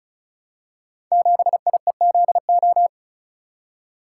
Morse code, heard as a keyed tone, about 36 wpm, 700 Hz. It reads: 7IEZO
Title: Code Groups - 5 Letter:Numbers - 35wpm